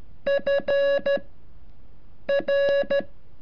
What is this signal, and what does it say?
Morse code, heard as a keyed tone, about 13 wpm, spaced as for 6 wpm, 560 Hz. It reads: FR
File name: curso / ejercicio 33